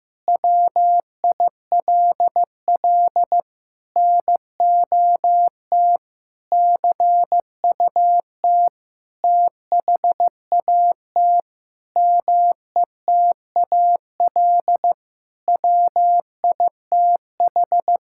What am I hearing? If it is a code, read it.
Morse code, heard as a keyed tone, about 15 wpm, 700 Hz. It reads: WILL NOT CUT THAT METAL WITH
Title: Sentences from Top 1000 Words - Review Part 6 - 15wpm